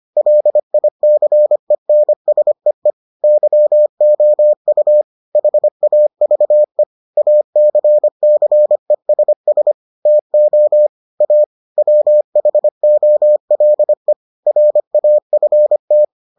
Morse code, heard as a keyed tone, about 25 wpm, 600 Hz. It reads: LICENSEE YOU HAVE ACCESS TO A WHOLE RAFT